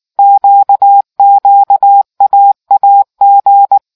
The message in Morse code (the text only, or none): QQAAG